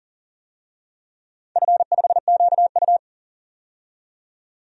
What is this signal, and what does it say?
Morse code, heard as a keyed tone, about 42 wpm, 700 Hz. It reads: F5QU
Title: DXCC-100 Call Signs - 4 Characters - 40wpm